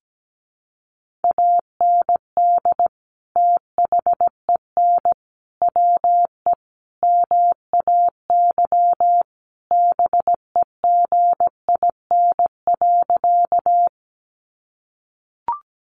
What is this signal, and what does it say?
Morse code, heard as a keyed tone, about 17 wpm, 700 Hz. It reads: AND THEN WE MAY BEGIN.
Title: Sentences from Top 300 Words 17wpm